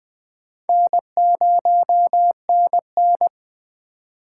Morse code, heard as a keyed tone, about 21 wpm, 700 Hz. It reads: N0NN
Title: Speed Racing - USA Call Signs 17wpm